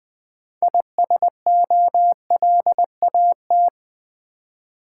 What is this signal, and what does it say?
Morse code, heard as a keyed tone, about 20 wpm, 700 Hz. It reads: ISOLAT
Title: Build A Word - 3 Letters At A Time - 8-Letter Words - 20wpm